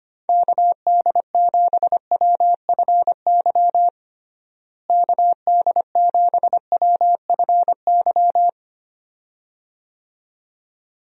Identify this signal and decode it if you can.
Morse code, heard as a keyed tone, about 25 wpm, 700 Hz. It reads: KD7WFY KD7WFY
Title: USA Call Signs - 6 Characters - Repeated 2x - 25wpm